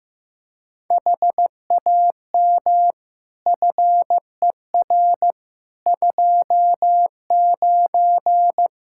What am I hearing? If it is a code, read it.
Morse code, heard as a keyed tone, about 15 wpm, 700 Hz. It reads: HAM FER 29
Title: Practice QSO - Exchange 3 - More Info - 15wpm